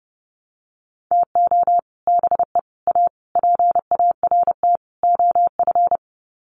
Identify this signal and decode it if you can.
Morse code, heard as a keyed tone, about 30 wpm, 700 Hz. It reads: TO BE A PART OF
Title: Sets of 5 Words 30wpm